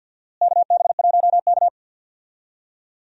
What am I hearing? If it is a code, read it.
Morse code, heard as a keyed tone, about 52 wpm, 700 Hz. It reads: KB1K
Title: USA Call Signs - 4 Characters - Repeated 3x - 50wpm